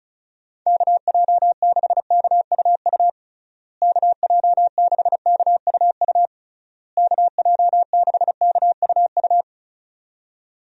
Morse code, heard as a keyed tone, about 34 wpm, 700 Hz. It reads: KJ6KUU KJ6KUU KJ6KUU
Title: USA Call Signs - 6 Characters - Repeated 3x - 35wpm